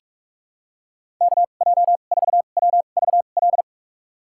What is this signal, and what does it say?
Morse code, heard as a keyed tone, about 46 wpm, spaced as for 33 wpm, 700 Hz. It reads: KJ4WVL
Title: USA Call Signs - 2x Character Spacing - 45wpm